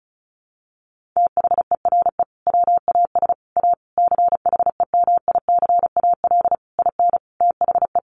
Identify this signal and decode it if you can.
Morse code, heard as a keyed tone, about 34 wpm, 700 Hz. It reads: THERE WAS A CHEMICAL IN THE
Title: Sentences from Top 1200 Words 35wpm